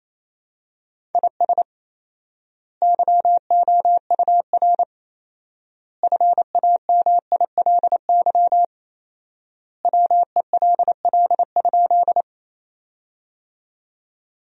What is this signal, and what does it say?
Morse code, heard as a keyed tone, about 28 wpm, 700 Hz. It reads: IS YOUR FAMILY WELL?
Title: 4x Word-Spacing - Sentences from Top 400 Words - 28wpm